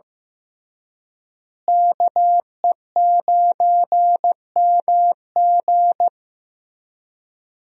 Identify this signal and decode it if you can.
Morse code, heard as a keyed tone, about 15 wpm, 700 Hz. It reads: KE9MG
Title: USA Call Signs - 5 Characters - Repeated 3x - 15wpm